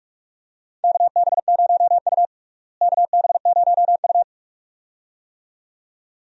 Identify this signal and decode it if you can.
Morse code, heard as a keyed tone, about 46 wpm, 700 Hz. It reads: KB0U KB0U
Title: USA Call Signs - 4 Characters - Repeated 2x - 45wpm